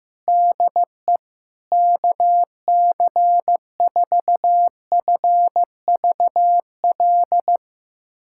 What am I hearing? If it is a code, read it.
Morse code, heard as a keyed tone, about 15 wpm, 700 Hz. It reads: DE KC4FVL
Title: USA Callsign DE Callsign - 6 Characters - 15wpm